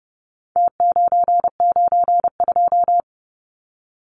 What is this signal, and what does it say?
Morse code, heard as a keyed tone, about 30 wpm, 700 Hz. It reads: T992
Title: Numbers - Contest Style - 4-Digit 30wpm